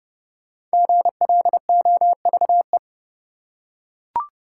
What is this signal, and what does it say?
Morse code, heard as a keyed tone, about 29 wpm, 700 Hz. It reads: GLOVE
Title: Five Letter Words - 30wpm